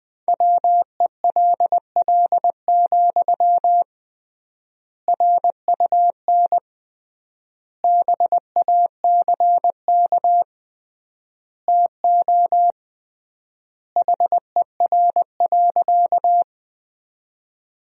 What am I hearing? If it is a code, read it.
Morse code, heard as a keyed tone, about 20 wpm, 700 Hz. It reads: WELL, RUN BACK TO HER.
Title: 3x Word-Spacing - Sentences from Top 300 Words - 20wpm